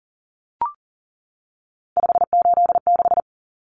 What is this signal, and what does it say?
Morse code, heard as a keyed tone, about 40 wpm, 700 Hz. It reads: 586
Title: Common Cut Numbers - 3-Digit 40wpm